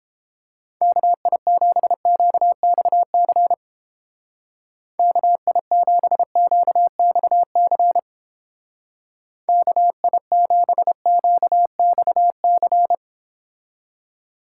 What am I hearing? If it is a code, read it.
Morse code, heard as a keyed tone, about 29 wpm, 700 Hz. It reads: KI7QXC KI7QXC KI7QXC